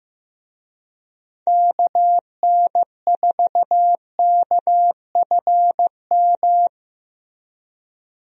Morse code, heard as a keyed tone, about 15 wpm, 700 Hz. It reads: KN4KFM
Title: USA Call Signs - 6 Characters - Repeated 3x - 15wpm